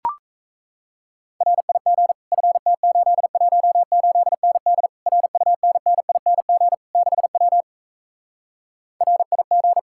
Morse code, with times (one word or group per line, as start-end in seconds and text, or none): 1.4-2.1 RIG
2.3-4.9 FT818ND
5.1-6.7 RUNNING
7.0-7.6 6W
9.0-9.8 RIG